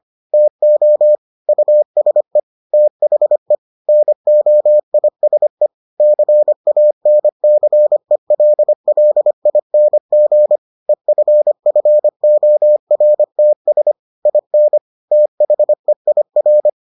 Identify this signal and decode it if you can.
Morse code, heard as a keyed tone, about 25 wpm, 600 Hz. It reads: TO USE THE NOISE CANCELLING EFFORTS IN THEIR